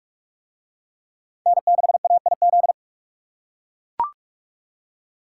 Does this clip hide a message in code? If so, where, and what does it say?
Yes: Morse code from 1.5 to 2.7 s, heard as a keyed tone, about 40 wpm, 700 Hz. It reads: N6AIZ